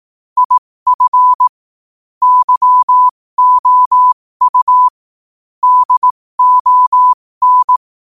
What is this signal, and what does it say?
Morse code, heard as a keyed tone, about 18 wpm, spaced as for 15 wpm, 1000 Hz. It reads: IF YOU DON
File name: qotdfiles / 15wpm